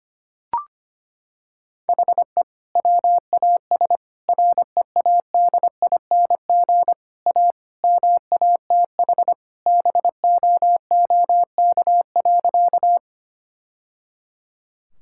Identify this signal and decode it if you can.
Morse code, heard as a keyed tone, about 25 wpm, 700 Hz. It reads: HE WAS READING A MATH BOOK.
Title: Sentences from Top 1000 Words - Review Part 6 - 25wpm